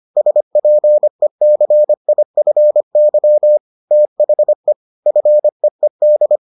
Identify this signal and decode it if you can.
Morse code, heard as a keyed tone, about 25 wpm, 600 Hz. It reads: SPECIFY THE FEED